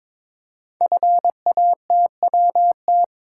Morse code, heard as a keyed tone, about 22 wpm, 700 Hz. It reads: FATWT
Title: Code Groups - 5 Letters - 22wpm